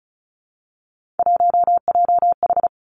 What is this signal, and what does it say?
Morse code, heard as a keyed tone, about 36 wpm, 700 Hz. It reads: 1JH
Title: Three Challenging Letter-Number Combinations 35wpm